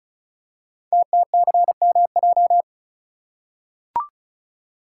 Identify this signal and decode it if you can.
Morse code, heard as a keyed tone, about 36 wpm, 690 Hz. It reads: TTCMJ
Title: Code Groups - 5 Letters - 35wpm